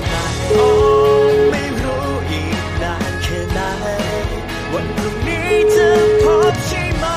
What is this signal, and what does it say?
Ringback tone (Europe, 425 Hz), with noise.